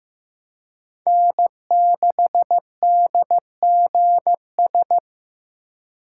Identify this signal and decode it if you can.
Morse code, heard as a keyed tone, about 16 wpm, 700 Hz. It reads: N6DGS
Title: USA Call Signs - 5 Characters - 4x Character Spacing - 15wpm